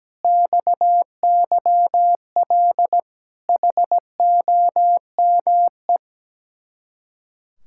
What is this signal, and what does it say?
Morse code, heard as a keyed tone, about 17 wpm, 700 Hz. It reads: XYL HOME
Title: Speed-Racing Common QSO Phrases - 17wpm